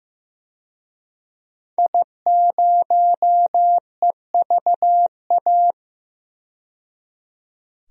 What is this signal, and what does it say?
Morse code, heard as a keyed tone, about 15 wpm, 700 Hz. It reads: I0EVA
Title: DXCC-100 Call Signs - 5 Characters - 15wpm